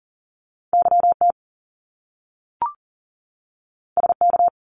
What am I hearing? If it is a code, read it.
Morse code, heard as a keyed tone, about 41 wpm, 700 Hz. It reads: YT SK
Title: Canadian Provinces and Territories 40wpm